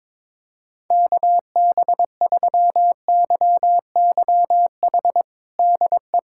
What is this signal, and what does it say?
Morse code, heard as a keyed tone, about 22 wpm, 700 Hz. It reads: KB3YYH DE
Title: USA Callsign DE Callsign - 6 Characters - 22wpm